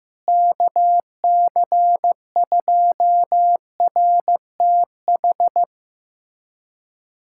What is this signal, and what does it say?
Morse code, heard as a keyed tone, about 16 wpm, 700 Hz. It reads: KC2RTH
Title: USA Callsign DE Callsign 15wpm